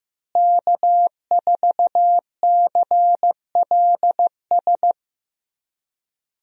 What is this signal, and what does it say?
Morse code, heard as a keyed tone, about 15 wpm, 700 Hz. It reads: K4CLS
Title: USA Call Signs - 5 Characters - Repeated 2x - 15wpm